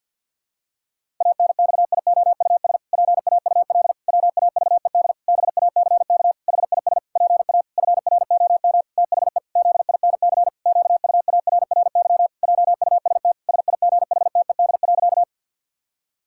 Morse code, heard as a keyed tone, about 50 wpm, 700 Hz. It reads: ANXIOUS PAUL WAVED BACK HIS PA FROM THE ZINC QUARRY JUST SIGHTED.
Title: Pangram Challenge v2 - 50wpm